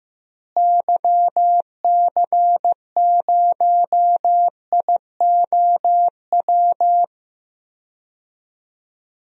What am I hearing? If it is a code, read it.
Morse code, heard as a keyed tone, about 15 wpm, 700 Hz. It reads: YC0IOW
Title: DXCC-100 Call Signs - 6 Characters - Repeated 3x - 15wpm